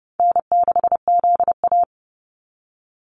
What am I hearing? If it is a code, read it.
Morse code, heard as a keyed tone, about 30 wpm, 700 Hz. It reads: N6ZA
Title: NA SOTA Activator Callsigns 30wpm